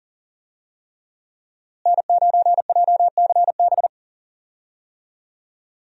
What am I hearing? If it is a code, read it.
Morse code, heard as a keyed tone, about 40 wpm, 700 Hz. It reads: N9JCB
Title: USA Call Signs - 5 Characters - Repeated 3x - 40wpm